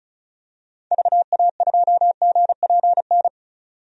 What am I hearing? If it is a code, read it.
Morse code, heard as a keyed tone, about 34 wpm, 700 Hz. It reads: VA2GPN
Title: DXCC-100 Call Signs - 6 Characters - 35wpm